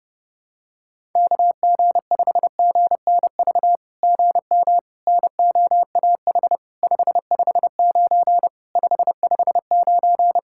Morse code, heard as a keyed tone, about 28 wpm, 700 Hz. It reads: KG5GNV GM NOAH 559 559
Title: SOTA QSO Practice 30wpm